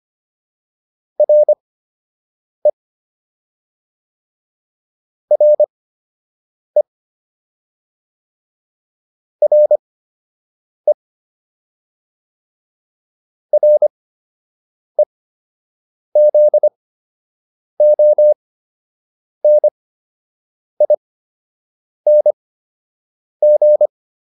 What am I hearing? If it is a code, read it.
Morse code, heard as a keyed tone, about 25 wpm, spaced as for 7 wpm, 600 Hz. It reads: RE RE RE REZONING
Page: fundamental-practice-files